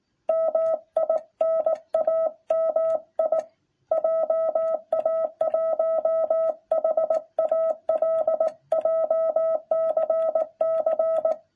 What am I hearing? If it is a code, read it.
Morse code, heard as a keyed tone, about 17 wpm, 660 Hz. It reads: MINAMI JA1HALJCC